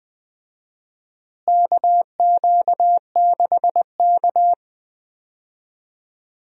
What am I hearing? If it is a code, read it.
Morse code, heard as a keyed tone, about 20 wpm, 700 Hz. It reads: KQ6K
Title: USA Call Signs - 4 Characters - Repeated 2x - 20wpm